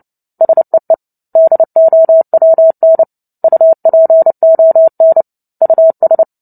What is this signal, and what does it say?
Morse code, heard as a keyed tone, about 30 wpm, 650 Hz. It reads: SEE DOWN UPON US
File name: Book Two / 03-Chapter 3 - 29 WPM